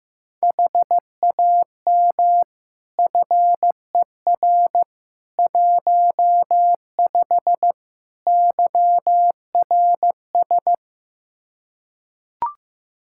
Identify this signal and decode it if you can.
Morse code, heard as a keyed tone, about 15 wpm, 700 Hz. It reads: HAM FER 15 YRS
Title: Practice QSO - Exchange 3 - More Info - 15wpm